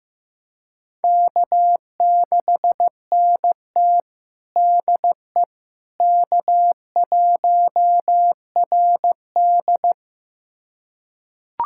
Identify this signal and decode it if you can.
Morse code, heard as a keyed tone, about 15 wpm, 700 Hz. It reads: K6NT DE K1RD
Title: USA Callsign DE Callsign - 4 Characters - 15wpm